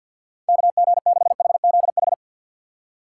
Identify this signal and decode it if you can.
Morse code, heard as a keyed tone, about 54 wpm, 700 Hz. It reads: KG6HZH
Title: USA Call Signs - 6 Characters - 6x Character Spacing - 50wpm